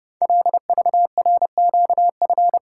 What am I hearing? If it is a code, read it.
Morse code, heard as a keyed tone, about 30 wpm, 700 Hz. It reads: LVRQF
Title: Code Groups - 5 Letters - 30wpm